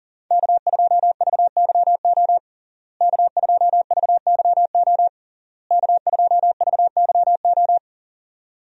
Morse code, heard as a keyed tone, about 40 wpm, 700 Hz. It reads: K2VYO K2VYO K2VYO
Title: USA Call Signs - 5 Characters - Repeated 3x - 40wpm